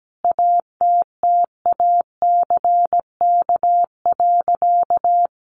Morse code, heard as a keyed tone, about 17 wpm, 700 Hz. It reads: ATTACK.
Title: Sentences from Top 1500 Words 17wpm